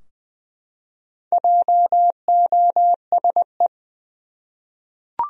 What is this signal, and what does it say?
Morse code, heard as a keyed tone, about 21 wpm, 700 Hz. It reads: JOSE
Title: USA Common Names - Max 5 Letters - 20wpm